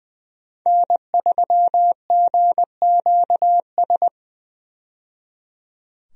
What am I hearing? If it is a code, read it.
Morse code, heard as a keyed tone, about 21 wpm, 700 Hz. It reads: N3GQS
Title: USA Call Signs - 5 Characters - 20wpm